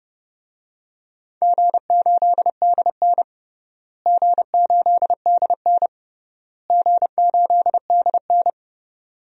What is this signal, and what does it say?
Morse code, heard as a keyed tone, about 31 wpm, 700 Hz. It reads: G8DN G8DN G8DN
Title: DXCC-100 Call Signs - 4 Characters - Repeated 3x - 30wpm